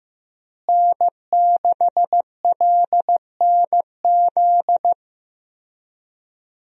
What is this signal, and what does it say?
Morse code, heard as a keyed tone, about 15 wpm, 700 Hz. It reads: N6LNZ
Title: USA Call Signs - 5 Characters - Repeated 2x - 15wpm